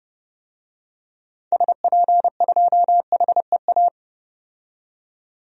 Morse code, heard as a keyed tone, about 30 wpm, 700 Hz. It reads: SP2HEA
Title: DXCC-100 Call Signs - 6 Characters - 30wpm